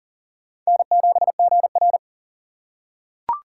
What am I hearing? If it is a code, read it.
Morse code, heard as a keyed tone, about 40 wpm, 700 Hz. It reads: N7GR
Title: USA Call Signs - 4 Characters - 2x Character Spacing - 40wpm